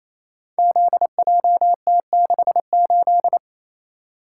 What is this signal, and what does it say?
Morse code, heard as a keyed tone, about 29 wpm, 690 Hz. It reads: ZJT68
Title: Code Groups - 5 Letter:Numbers - 28wpm